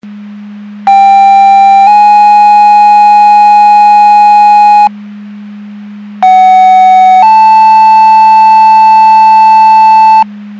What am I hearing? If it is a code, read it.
Two-tone page, heard as a one-second tone, then a three-second tone, about 790 then 830 Hz, about 750 then 880 Hz. It is heard from 0.9 to 10.2 s.
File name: Portage County 154.3850 Fire EMS Paging